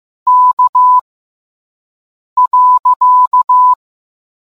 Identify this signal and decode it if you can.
Morse code, heard as a keyed tone, about 15 wpm, spaced as for 5 wpm, 1000 Hz. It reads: K.